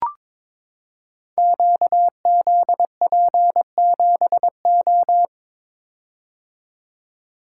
Morse code, heard as a keyed tone, about 22 wpm, 700 Hz. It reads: QZP7O
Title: Code Groups - 5 Letter:Numbers - 22wpm